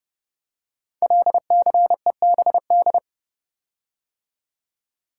Morse code, heard as a keyed tone, about 28 wpm, 700 Hz. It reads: LCEBD
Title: Code Groups - 5 Letters - 30wpm